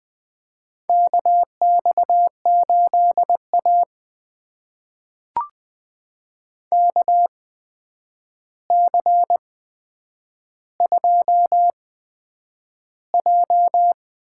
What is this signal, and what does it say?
Morse code, heard as a keyed tone, about 20 wpm, spaced as for 9 wpm, 700 Hz. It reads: KX8A KC2J
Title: USA Call Signs - 4 Characters - 8x Character Spacing - 20wpm